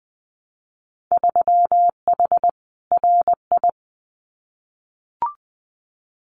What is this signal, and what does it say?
Morse code, heard as a keyed tone, about 20 wpm, 700 Hz. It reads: 3H RI
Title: Winter Field Day Exchanges Encore 20wpm